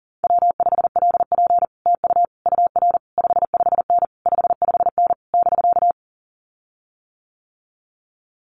Morse code, heard as a keyed tone, about 37 wpm, 700 Hz. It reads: W5LP TU UR 55N 55N <BK>
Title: Common QSO Phrases Encore 40wpm